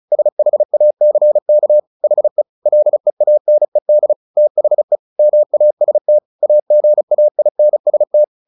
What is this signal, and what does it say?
Morse code, heard as a keyed tone, about 34 wpm, 600 Hz. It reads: SHACK HE LEANED THE MAST AGAINST
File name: the old man and the sea 2 35wpm